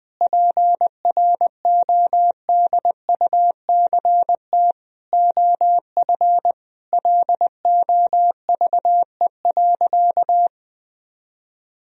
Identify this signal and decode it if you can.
Morse code, heard as a keyed tone, about 20 wpm, 700 Hz. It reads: PRODUCT OF LOVE.
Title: Sentences from Top 1000 Words - Review Part 2 - 20wpm